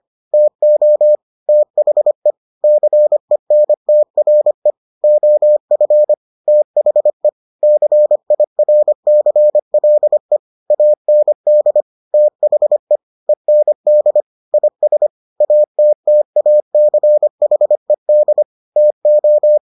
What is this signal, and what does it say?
Morse code, heard as a keyed tone, about 25 wpm, 600 Hz. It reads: TO THE CENTRE OF THE CIRCLE AND THE END IS ATTACHED TO